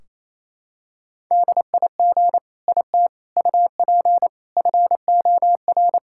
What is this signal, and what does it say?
Morse code, heard as a keyed tone, about 28 wpm, 700 Hz. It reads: DIG IT UP FOR
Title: Sentences from Top 1600 Words 28wpm